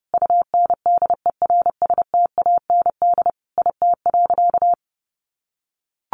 Morse code, heard as a keyed tone, about 30 wpm, 700 Hz. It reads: UNDERSTAND IT.
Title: Sentences from Top 1400 Words 30wpm